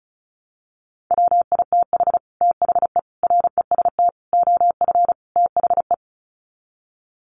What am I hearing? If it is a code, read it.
Morse code, heard as a keyed tone, about 34 wpm, 700 Hz. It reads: WITH THE REST OF THE
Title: Sets of 5 Words 35wpm